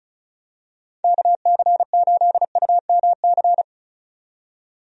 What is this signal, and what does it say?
Morse code, heard as a keyed tone, about 36 wpm, 700 Hz. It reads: KC8UMC
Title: USA Call Signs - 6 Characters - 35wpm